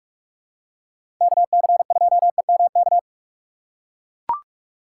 Morse code, heard as a keyed tone, about 42 wpm, 700 Hz. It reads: KC2EMK
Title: USA Call Signs - 6 Characters - 2x Character Spacing - 45wpm